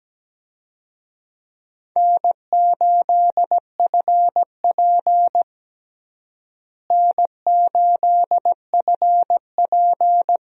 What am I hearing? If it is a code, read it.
Morse code, heard as a keyed tone, about 17 wpm, 700 Hz. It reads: N8FP N8FP
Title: USA Call Signs - 4 Characters - Repeated 3x - 17wpm